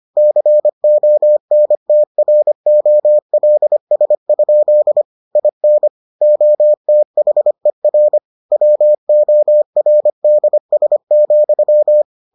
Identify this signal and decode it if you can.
Morse code, heard as a keyed tone, about 25 wpm, 600 Hz. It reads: CONTROLS? IN OTHER WORDS,